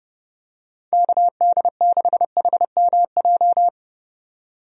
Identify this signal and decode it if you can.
Morse code, heard as a keyed tone, about 30 wpm, 700 Hz. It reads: KD6HMJ
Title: USA Call Signs - 6 Characters - 30wpm